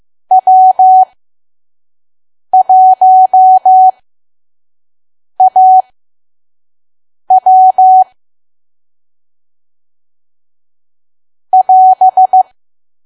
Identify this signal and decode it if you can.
Morse code, heard as a keyed tone, about 15 wpm, spaced as for 5 wpm, 750 Hz. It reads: W1AW <AS>
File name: Archive / 5 WPM files / PM